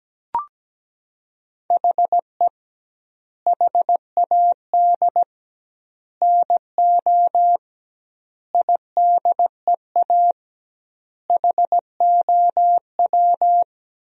Morse code, heard as a keyed tone, about 17 wpm, 700 Hz. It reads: HE HAD NO IDEA HOW
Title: 2x Word-Spacing - Sentences from Top 300 Words - 17wpm